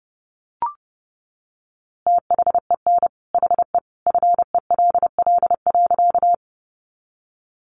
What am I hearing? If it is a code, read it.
Morse code, heard as a keyed tone, about 30 wpm, 700 Hz. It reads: THEN HE FELL.
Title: A Sound Of Thunder 30wpm